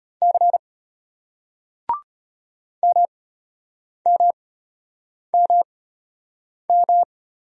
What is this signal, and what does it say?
Morse code, heard as a keyed tone, about 33 wpm, spaced as for 8 wpm, 700 Hz. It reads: C MMMM